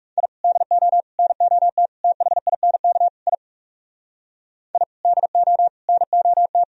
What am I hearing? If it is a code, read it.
Morse code, heard as a keyed tone, about 42 wpm, 700 Hz. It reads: I DO NOT THINK I I DO NOT